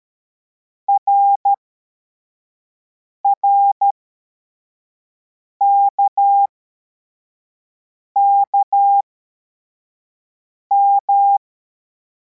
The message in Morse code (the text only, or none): RRKKM